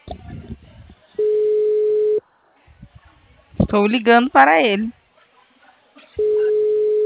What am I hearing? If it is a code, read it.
Ringback tone (Europe, 425 Hz), with speech.
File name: out- -2006- - - .6